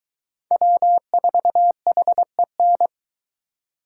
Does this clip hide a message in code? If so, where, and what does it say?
Yes: Morse code from 0.5 to 2.9 s, heard as a keyed tone, about 22 wpm, 700 Hz. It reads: W4HEN